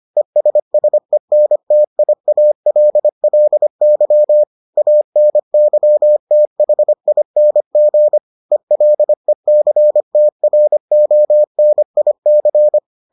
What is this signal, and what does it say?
Morse code, heard as a keyed tone, about 25 wpm, 600 Hz. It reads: ESSENTIALLY ANYTHING ELECTRONIC